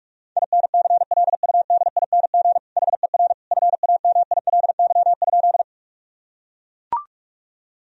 Morse code, heard as a keyed tone, about 41 wpm, 700 Hz. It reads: INCLUDING HER FAMILY?
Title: Sentences from Top 1000 Words - Review Part 4 - 45wpm